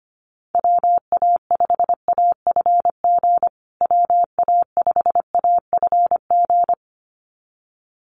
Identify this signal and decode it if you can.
Morse code, heard as a keyed tone, about 25 wpm, 700 Hz. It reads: WA5AFG WA5AFG
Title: USA Call Signs Repeated 2x 25wpm